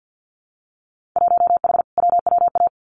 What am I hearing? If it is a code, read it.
Morse code, heard as a keyed tone, about 50 wpm, 700 Hz. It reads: 1H WWA
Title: Winter Field Day Exchanges 50wpm